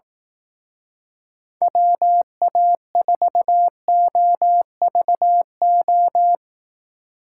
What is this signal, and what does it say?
Morse code, heard as a keyed tone, about 17 wpm, 700 Hz. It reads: WA4OVO
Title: Speed Racing - USA 2x3 Call Signs 15wpm